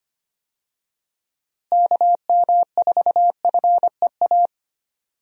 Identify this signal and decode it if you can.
Morse code, heard as a keyed tone, about 25 wpm, 700 Hz. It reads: KM4FEA
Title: USA Call Signs - 6 Characters - Repeated 3x - 25wpm